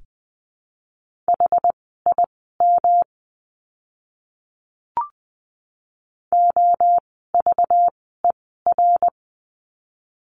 Morse code, heard as a keyed tone, about 20 wpm, spaced as for 14 wpm, 700 Hz. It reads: HIM OVER
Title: 237 Lesson 38f - Letter q - Words 20wpm